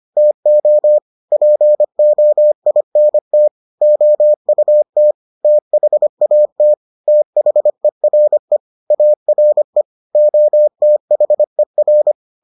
Morse code, heard as a keyed tone, about 25 wpm, 600 Hz. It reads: TO POINT OUT THAT THERE ARE OTHER